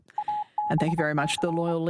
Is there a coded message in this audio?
Yes: Morse code from 0.2 to 1.7 s, heard as a keyed tone, about 24 wpm, 840 Hz, with speech. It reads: ABC